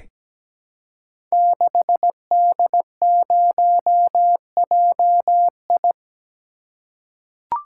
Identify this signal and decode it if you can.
Morse code, heard as a keyed tone, about 17 wpm, 700 Hz. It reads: 6D0JI
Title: Code Groups - 5 Letter:Numbers - 17wpm